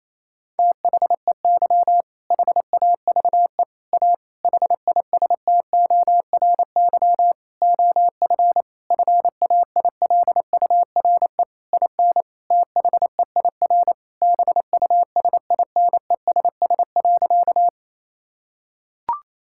Morse code, heard as a keyed tone, about 27 wpm, 700 Hz. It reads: THEY HAVE A HISTORY OF FAILURE IN THEIR BUSINESS.
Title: Sentences from Top 1000 Words - Review Part 6 - 28wpm